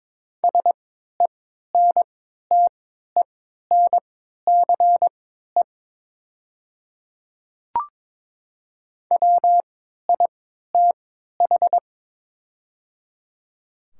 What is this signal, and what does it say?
Morse code, heard as a keyed tone, about 22 wpm, spaced as for 12 wpm, 700 Hz. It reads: SENTENCE WITH
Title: 3x Character Spacing - Top 500 Words - 22wpm